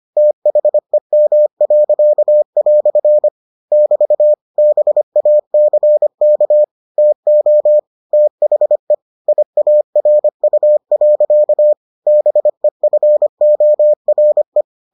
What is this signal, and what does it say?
Morse code, heard as a keyed tone, about 25 wpm, 600 Hz. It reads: THEM." <BT> BACK TO THE IARU. BEFORE